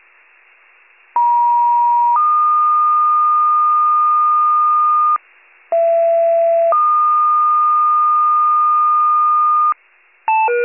Two-tone page, heard as a one-second tone, then a three-second tone, about 950 then 1220 Hz, about 670 then 1220 Hz. It is heard from 1.2 to 9.7 s.